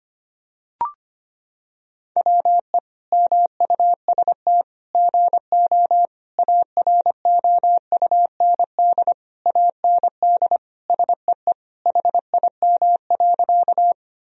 Morse code, heard as a keyed tone, about 25 wpm, 700 Hz. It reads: WE MUST GO AROUND AND SEE HIM.
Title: Sentences from Top 1000 Words - Review Part 3 - 25wpm